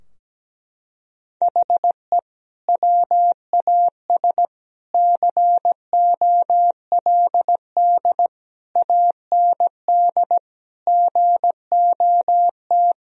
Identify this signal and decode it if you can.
Morse code, heard as a keyed tone, about 17 wpm, 700 Hz. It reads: HE WAS COLD AND GOT
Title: Sentences from Top 1000 Words - Review Part 4 - 17wpm